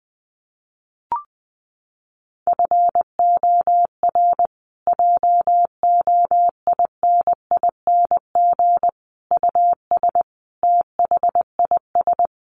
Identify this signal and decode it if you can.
Morse code, heard as a keyed tone, about 19 wpm, 700 Hz. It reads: FOR JOINING US THIS
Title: Sets of 5 Words Encore 20wpm